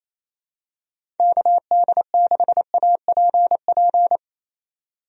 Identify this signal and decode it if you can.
Morse code, heard as a keyed tone, about 28 wpm, 700 Hz. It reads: KD6APP
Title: USA Call Signs - 6 Characters - 28wpm